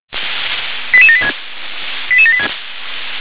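Five-tone selective call (ZVEI), heard as quick runs of five tones, tones of about 70 ms. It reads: 88786 88786
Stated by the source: ZVEI five-tone selective call received on an SDR.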